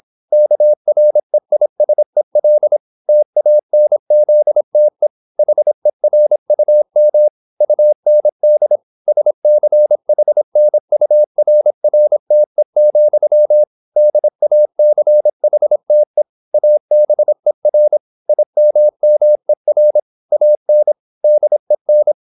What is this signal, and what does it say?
Morse code, heard as a keyed tone, about 26 wpm, 600 Hz. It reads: KREISEL TANZTE HERUM UND SCHNURRTE, DACHTE ABER IMMER AN DEN